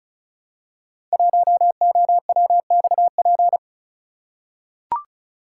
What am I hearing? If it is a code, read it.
Morse code, heard as a keyed tone, about 36 wpm, 700 Hz. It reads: 1OWXP
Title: Code Groups - 5 Letter:Numbers - 35wpm